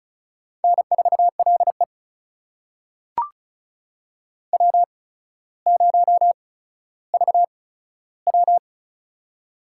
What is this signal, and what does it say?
Morse code, heard as a keyed tone, about 36 wpm, spaced as for 15 wpm, 700 Hz. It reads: N4LE W0VW
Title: USA Call Signs - 4 Characters - 8x Character Spacing - 35wpm